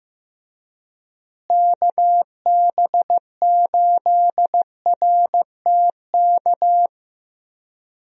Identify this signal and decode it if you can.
Morse code, heard as a keyed tone, about 16 wpm, 700 Hz. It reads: KB8RTK
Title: USA Call Signs - 6 Characters - Repeated 2x - 15wpm